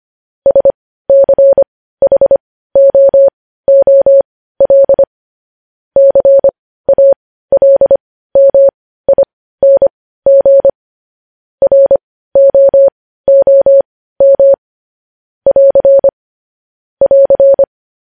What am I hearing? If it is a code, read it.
Morse code, heard as a keyed tone, about 25 wpm, spaced as for 15 wpm, 550 Hz. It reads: SCHOOL CALMING ROOM <AR> <AR>
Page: news-headlines-in-morse-code-at-15-wpm